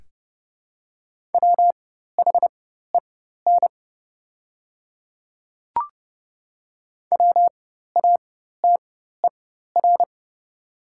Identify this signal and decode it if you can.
Morse code, heard as a keyed tone, about 30 wpm, spaced as for 14 wpm, 690 Hz. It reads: WHEN WATER
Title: 4x Character Spacing - Top 100 Words - 30wpm